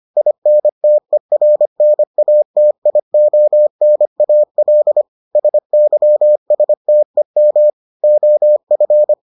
Morse code, heard as a keyed tone, about 25 wpm, 600 Hz. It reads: INTERNATIONAL SYSTEM OF